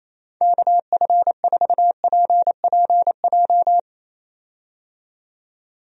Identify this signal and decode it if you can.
Morse code, heard as a keyed tone, about 28 wpm, 700 Hz. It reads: KF4PPJ
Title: USA Call Signs - 6 Characters - 28wpm